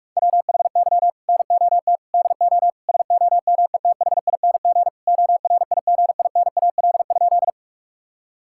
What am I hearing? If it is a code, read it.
Morse code, heard as a keyed tone, about 46 wpm, 700 Hz. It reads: WHY NOT DO SOMETHING ORIGINAL?
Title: Sentences from Top 1000 Words - Review Part 6 - 45wpm